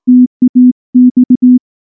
Morse code, heard as a keyed tone, about 19 wpm, 250 Hz. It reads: TAX
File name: morse / morse-training-20WPM-9